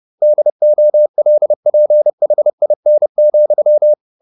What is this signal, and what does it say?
Morse code, heard as a keyed tone, about 30 wpm, 600 Hz. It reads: DOLPHIN,
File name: the old man and the sea / the old man and the sea 1 30wpm